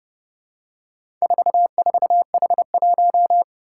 Morse code, heard as a keyed tone, about 30 wpm, 700 Hz. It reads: 44H1
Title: Four Challenging Letter-Number Combinations 30wpm